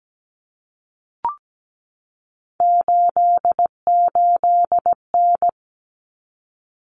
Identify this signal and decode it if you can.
Morse code, heard as a keyed tone, about 17 wpm, 700 Hz. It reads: 88N
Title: Common Cut Numbers - 3-Digit 17wpm